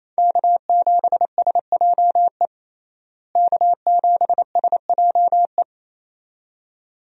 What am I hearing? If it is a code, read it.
Morse code, heard as a keyed tone, about 28 wpm, 700 Hz. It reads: K7SJE K7SJE
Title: USA Call Signs - 5 Characters - Repeated 2x - 28wpm